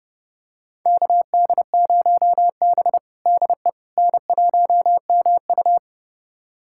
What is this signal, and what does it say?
Morse code, heard as a keyed tone, about 30 wpm, 700 Hz. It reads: KD0B DE N1MU
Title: USA Callsign DE Callsign - 4 Characters - 30wpm